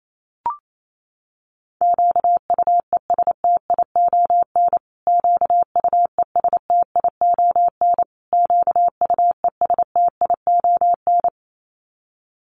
Morse code, heard as a keyed tone, about 28 wpm, 700 Hz. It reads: QUESTION QUESTION QUESTION
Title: Top 500 Words Repeated 28wpm